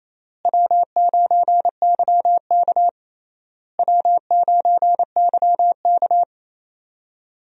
Morse code, heard as a keyed tone, about 28 wpm, 700 Hz. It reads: W9YK W9YK
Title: USA Call Signs - 4 Characters - Repeated 3x - 28wpm